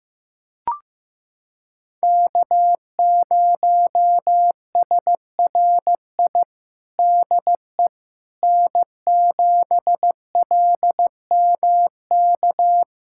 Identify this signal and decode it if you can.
Morse code, heard as a keyed tone, about 15 wpm, 700 Hz. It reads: K0SRI DE N7LMK
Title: USA Callsign DE Callsign - 5 Characters - 15wpm